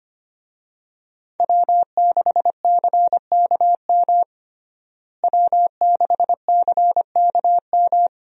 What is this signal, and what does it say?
Morse code, heard as a keyed tone, about 25 wpm, 700 Hz. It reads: W6CKM W6CKM
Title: USA Call Signs - 5 Characters - Repeated 3x - 25wpm